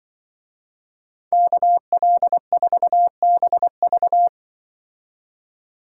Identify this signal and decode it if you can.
Morse code, heard as a keyed tone, about 23 wpm, 700 Hz. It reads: KL4BV
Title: Speed Racing - USA 2x2 Call Signs 20wpm